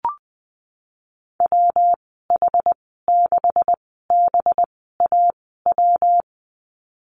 Morse code, heard as a keyed tone, about 20 wpm, spaced as for 15 wpm, 700 Hz. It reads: WH6BAW
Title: 193 Lesson 31g - Letter v - Call Signs 20wpm